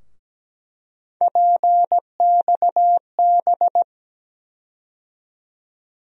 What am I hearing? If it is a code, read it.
Morse code, heard as a keyed tone, about 17 wpm, 700 Hz. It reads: PXB
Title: USA Call Sign Postfixes - 3 Letters 17wpm